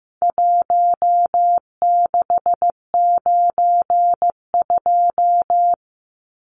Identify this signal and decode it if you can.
Morse code, heard as a keyed tone, about 15 wpm, 700 Hz. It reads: 1692
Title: Numbers - 4-Digit 15wpm